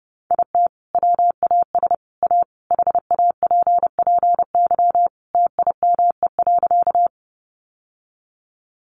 Morse code, heard as a keyed tone, about 30 wpm, 700 Hz. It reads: IT WAS A HAPPY TIME.
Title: Sentences from Top 1000 Words 30wpm